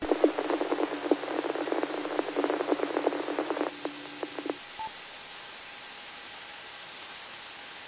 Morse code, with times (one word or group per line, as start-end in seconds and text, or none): none